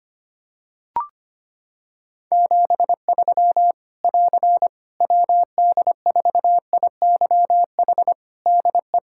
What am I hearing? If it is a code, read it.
Morse code, heard as a keyed tone, about 25 wpm, 700 Hz. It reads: 73 <AR> WD4IYH DE
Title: Practice QSO - Exchange 4 - Ending - 25wpm